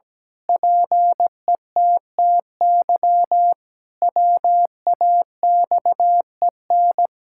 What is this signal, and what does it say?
Morse code, heard as a keyed tone, about 17 wpm, 700 Hz. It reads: PETTY WAXEN
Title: Pangram Challenge v2 - 17wpm